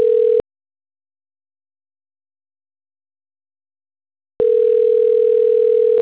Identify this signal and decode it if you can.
Ringback tone (North America), 440 + 480 Hz.
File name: autotestagent / 11-19-04.136..callcentertransfer. .9bfde4c3-06f0-4e0e-95a7